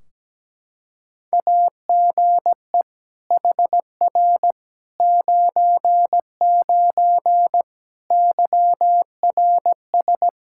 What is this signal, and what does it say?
Morse code, heard as a keyed tone, about 17 wpm, 700 Hz. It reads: AGE HR 99 YRS
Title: Practice QSO - Exchange 3 - More Info - 17wpm